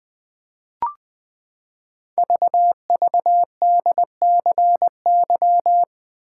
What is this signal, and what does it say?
Morse code, heard as a keyed tone, about 20 wpm, 700 Hz. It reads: VVDCY
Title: Code Groups - 5 Letters - 20wpm